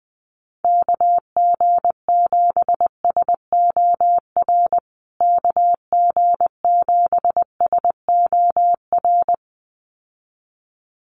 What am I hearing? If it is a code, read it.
Morse code, heard as a keyed tone, about 20 wpm, 700 Hz. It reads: KG7SOR KG7SOR
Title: USA Call Signs Repeated 2x 20wpm